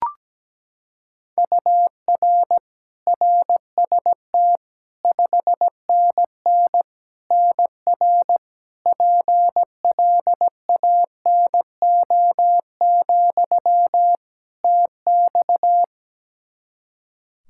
Morse code, heard as a keyed tone, about 17 wpm, 700 Hz. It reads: UR RST 5NN NR PLANO, TX
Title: Practice QSO - Exchange 1 - Beginning - 17wpm